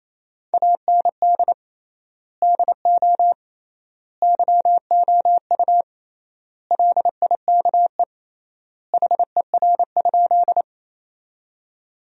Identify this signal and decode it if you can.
Morse code, heard as a keyed tone, about 28 wpm, 700 Hz. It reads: AND DO YOU LIKE HER?
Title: 3x Word-Spacing - Sentences from Top 100 Words - 28wpm